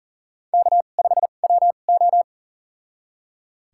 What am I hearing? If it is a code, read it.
Morse code, heard as a keyed tone, about 40 wpm, spaced as for 29 wpm, 700 Hz. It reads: K5WO
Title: USA Call Signs - 4 Characters - 2x Character Spacing - 40wpm